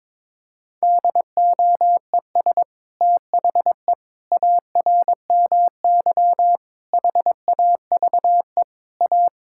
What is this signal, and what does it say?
Morse code, heard as a keyed tone, about 22 wpm, 700 Hz. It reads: DOES THE ARMY HAVE A
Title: Sentences from Top 1000 Words - Review Part 5 - 22wpm